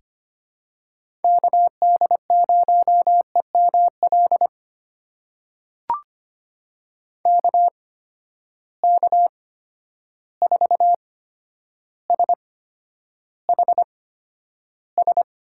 Morse code, heard as a keyed tone, about 25 wpm, spaced as for 11 wpm, 700 Hz. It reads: KD0EML KK4SHS
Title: USA Call Signs - 6 Characters - 8x Character Spacing - 25wpm